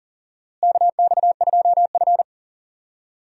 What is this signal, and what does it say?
Morse code, heard as a keyed tone, about 37 wpm, 700 Hz. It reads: KX2F
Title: USA Call Signs - 4 Characters - Repeated 2x - 40wpm